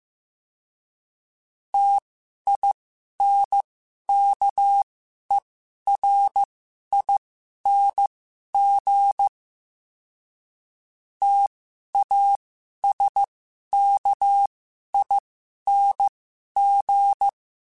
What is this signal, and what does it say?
Morse code, heard as a keyed tone, about 15 wpm, spaced as for 11 wpm, 800 Hz. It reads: TINKERING TASKING
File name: Special INGs-mp3s / Special INGs-mp3s / Sngl Ing @10